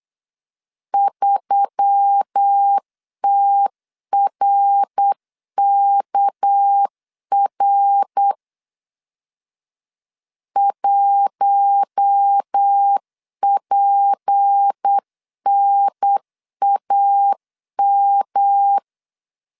Morse code, heard as a keyed tone, about 8 wpm, 800 Hz. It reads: 3TRKR 1PNAM
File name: CW / DET / 1 6 1500C S C 8wpm 5219